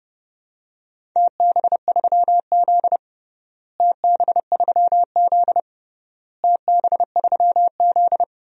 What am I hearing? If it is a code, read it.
Morse code, heard as a keyed tone, about 30 wpm, 700 Hz. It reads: TB3Z TB3Z TB3Z
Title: DXCC-100 Call Signs - 4 Characters - Repeated 3x - 30wpm